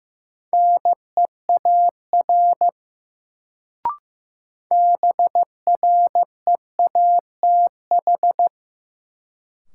Morse code, heard as a keyed tone, about 15 wpm, 700 Hz. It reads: NEAR BREATH
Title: Sets of 5 Words Encore - New Word Drill - 15wpm